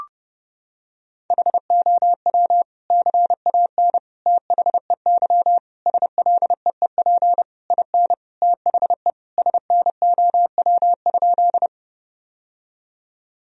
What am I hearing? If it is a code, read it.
Morse code, heard as a keyed tone, about 30 wpm, 700 Hz. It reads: HOW CAN THEY SLEEP IN THE SNOW?
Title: Sentences from Top 1000 Words - Review Part 3 - 30wpm